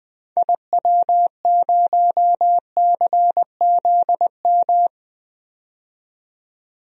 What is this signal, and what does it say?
Morse code, heard as a keyed tone, about 20 wpm, 700 Hz. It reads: IW0CZM
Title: DXCC-100 Call Signs - 6 Characters - 20wpm